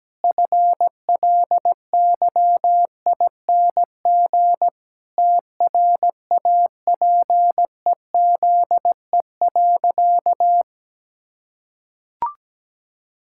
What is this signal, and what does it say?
Morse code, heard as a keyed tone, about 17 wpm, 700 Hz. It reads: FLYING TRAPEZE.
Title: Pangram Challenge v2 - 17wpm